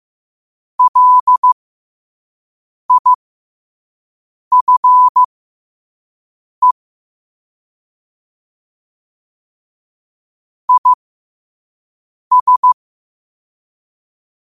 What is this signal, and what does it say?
Morse code, heard as a keyed tone, about 15 wpm, spaced as for 5 wpm, 1000 Hz. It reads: LIFE IS
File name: qotdfiles / 5wpm